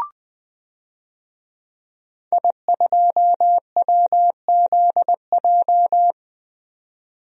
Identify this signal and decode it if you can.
Morse code, heard as a keyed tone, about 20 wpm, 700 Hz. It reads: I2WZJ
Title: DXCC-100 Call Signs - 5 Characters - 20wpm